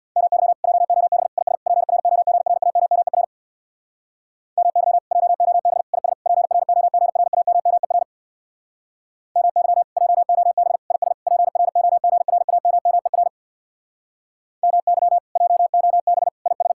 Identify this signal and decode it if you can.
Morse code, heard as a keyed tone, about 64 wpm, 700 Hz. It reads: MY JOB IS PROGRAMMER MY JOB IS PROGRAMMER MY JOB IS PROGRAMMER MY JOB IS